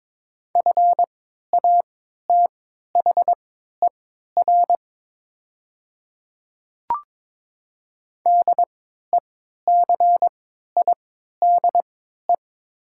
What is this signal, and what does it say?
Morse code, heard as a keyed tone, about 22 wpm, spaced as for 12 wpm, 700 Hz. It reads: FATHER DECIDE
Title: 3x Character Spacing - Top 500 Words - 22wpm